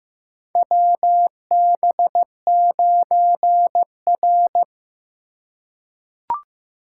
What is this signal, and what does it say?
Morse code, heard as a keyed tone, about 15 wpm, 700 Hz. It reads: WB9R
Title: USA Call Signs - 4 Characters - 8x Character Spacing - 15wpm